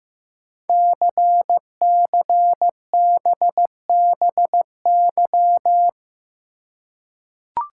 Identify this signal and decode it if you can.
Morse code, heard as a keyed tone, about 15 wpm, 700 Hz. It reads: CCBBY
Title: Code Groups - 5 Letters - 15wpm